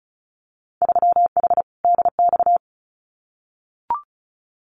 Morse code, heard as a keyed tone, about 36 wpm, 700 Hz. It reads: 3H DX
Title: Winter Field Day Exchanges Encore 35wpm